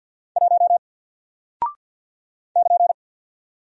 Morse code, heard as a keyed tone, about 50 wpm, 700 Hz. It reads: 1 <KN>
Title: Single Character (Letter:Number:Punctuation:Prosign) 50wpm